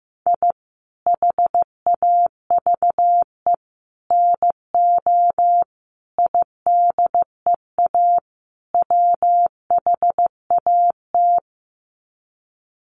Morse code, heard as a keyed tone, about 15 wpm, 700 Hz. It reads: I HAVE NO IDEA WHAT
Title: Sets of 5 Words Encore 15wpm